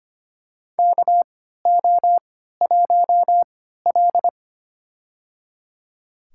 Morse code, heard as a keyed tone, about 25 wpm, spaced as for 14 wpm, 700 Hz. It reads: KO1L
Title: USA Call Signs - 4 Characters - 3x Character Spacing - 25wpm